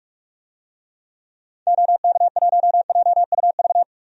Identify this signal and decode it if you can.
Morse code, heard as a keyed tone, about 46 wpm, 700 Hz. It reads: OK1JUV
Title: DXCC-100 Call Signs - 6 Characters - 45wpm